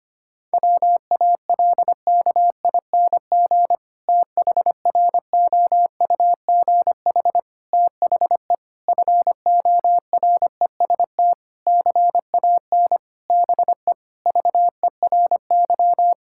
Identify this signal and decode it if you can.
Morse code, heard as a keyed tone, about 25 wpm, 700 Hz. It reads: WALKING THROUGH THE FOREST CAN BE VERY
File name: practice / 25wpm / Difficult Sentences with Hints - 25wpm